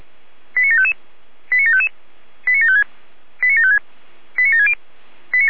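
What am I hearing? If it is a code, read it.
Five-tone selective call (ZVEI), heard as quick runs of five tones, tones of about 70 ms. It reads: 89755 89755 89756 89756 89760
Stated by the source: ZVEI five-tone selective call received by SDR.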